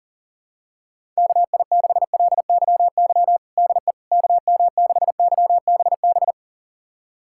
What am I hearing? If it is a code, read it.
Morse code, heard as a keyed tone, about 40 wpm, 700 Hz. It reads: KI6LYY DE KM6YBB
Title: USA Callsign DE Callsign - 6 Characters - 40wpm